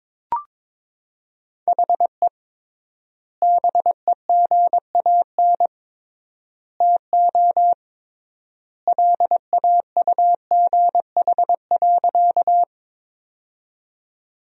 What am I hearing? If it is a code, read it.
Morse code, heard as a keyed tone, about 22 wpm, 700 Hz. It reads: HE BEGAN TO LAUGH.
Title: 3x Word-Spacing - Sentences from Top 500 Words - 22wpm